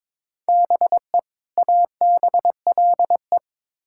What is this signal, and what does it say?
Morse code, heard as a keyed tone, about 22 wpm, 700 Hz. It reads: BE ABLE
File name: Build A Phrase - 2-Word Phrases - 22wpm